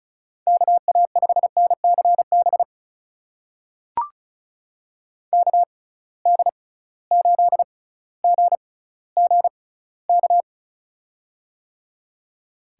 Morse code, heard as a keyed tone, about 34 wpm, spaced as for 18 wpm, 700 Hz. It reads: KA5NCB KD8GGK
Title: USA Call Signs - 6 Characters - 6x Character Spacing - 35wpm